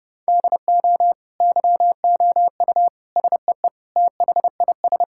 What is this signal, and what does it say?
Morse code, heard as a keyed tone, about 30 wpm, 700 Hz. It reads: DO YOU SEE THIS
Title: Sentences from Top 1600 Words 30wpm